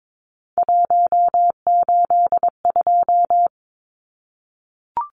Morse code, heard as a keyed tone, about 22 wpm, 700 Hz. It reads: 182
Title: Numbers - 3-Digit 22wpm